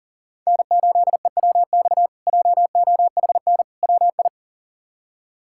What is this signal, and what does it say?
Morse code, heard as a keyed tone, about 40 wpm, 700 Hz. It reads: N8EWX JOHN WI
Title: K1USN SST QSO Practice - 40wpm